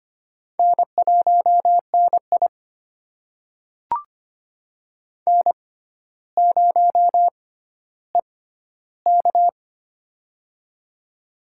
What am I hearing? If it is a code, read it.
Morse code, heard as a keyed tone, about 25 wpm, spaced as for 13 wpm, 700 Hz. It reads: N1NI N0EK
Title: USA Call Signs - 4 Characters - 6x Character Spacing - 25wpm